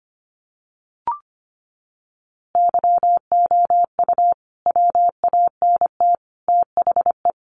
Morse code, heard as a keyed tone, about 25 wpm, 700 Hz. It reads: YOU WANT THE
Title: A Sound Of Thunder 25wpm